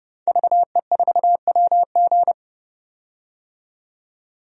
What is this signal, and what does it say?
Morse code, heard as a keyed tone, about 30 wpm, 700 Hz. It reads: VE4WG
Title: DXCC-100 Call Signs - 5 Characters - 30wpm